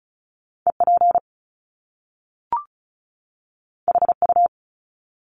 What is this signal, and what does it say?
Morse code, heard as a keyed tone, about 36 wpm, 700 Hz. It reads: EP HU
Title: Top Two Letter Combinations in Words 35wpm